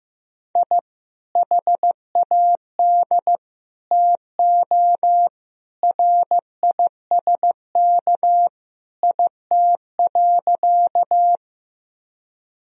Morse code, heard as a keyed tone, about 15 wpm, 700 Hz. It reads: I HAD TO RISK IT.
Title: Sentences from Top 1000 Words - Review Part 4 - 15wpm